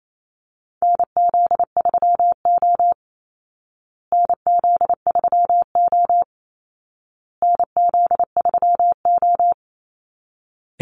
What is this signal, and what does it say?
Morse code, heard as a keyed tone, about 28 wpm, 700 Hz. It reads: NZ3O NZ3O NZ3O
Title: Call Signs Repeated 2x - 28wpm